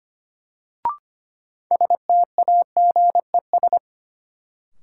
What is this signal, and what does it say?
Morse code, heard as a keyed tone, about 25 wpm, 700 Hz. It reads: STAGES
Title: Sets of 5 Words Encore - New Word Drill - 25wpm